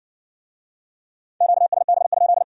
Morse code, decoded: QSB?

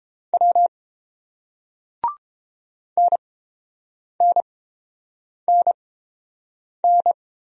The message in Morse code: W NNNN

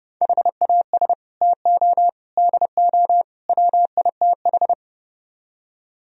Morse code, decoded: HAS TO DO WITH

